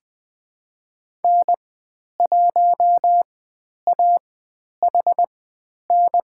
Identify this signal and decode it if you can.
Morse code, heard as a keyed tone, about 20 wpm, spaced as for 10 wpm, 700 Hz. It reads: N1AHN
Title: 095 Lesson 16g - Letter c - Call Signs 20wpm